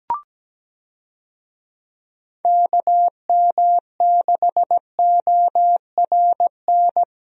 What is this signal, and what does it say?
Morse code, heard as a keyed tone, about 17 wpm, 700 Hz. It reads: KM6ORN
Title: USA Call Signs - 6 Characters - Repeated 2x - 17wpm